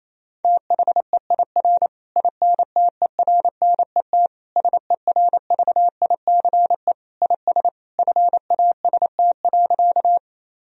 Morse code, heard as a keyed tone, about 28 wpm, 700 Hz. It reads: THEIR INTERNET SERVICE IS FAST.
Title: Sentences from Top 1000 Words - Review Part 4 - 28wpm